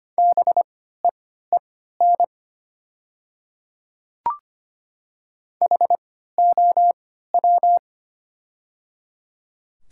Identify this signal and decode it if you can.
Morse code, heard as a keyed tone, about 25 wpm, spaced as for 14 wpm, 690 Hz. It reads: BEEN HOW
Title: 3x Character Spacing - Top 100 Words - 25wpm